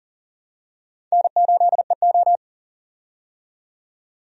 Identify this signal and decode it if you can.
Morse code, heard as a keyed tone, about 40 wpm, 700 Hz. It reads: N8EO
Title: USA Call Signs - 4 Characters - 40wpm